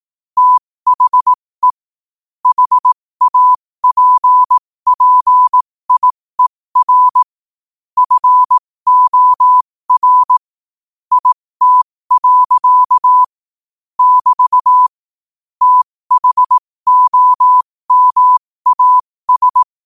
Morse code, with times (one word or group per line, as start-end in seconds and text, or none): 0.4-1.7 THE
2.5-7.2 HAPPIER
8.0-10.4 FOR
11.1-13.2 IT.
14.0-14.9 <BT>
15.6-19.6 THOMAS